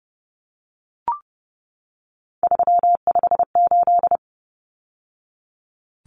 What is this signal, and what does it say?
Morse code, heard as a keyed tone, about 30 wpm, 700 Hz. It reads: 358
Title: Numbers - 3-Digit 30wpm